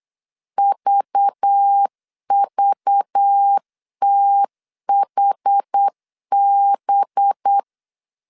Morse code, heard as a keyed tone, about 8 wpm, 800 Hz. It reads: VVTHB
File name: 8wpm